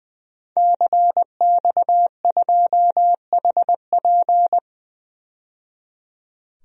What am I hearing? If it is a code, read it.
Morse code, heard as a keyed tone, about 20 wpm, 700 Hz. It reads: CX2HP
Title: DXCC-100 Call Signs - 5 Characters - 20wpm